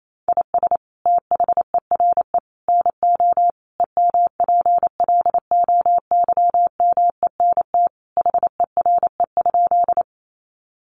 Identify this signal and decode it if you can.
Morse code, heard as a keyed tone, about 28 wpm, 700 Hz. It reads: IS THERE NO EMPLOYMENT HERE?
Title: Sentences from Top 1500 Words 28wpm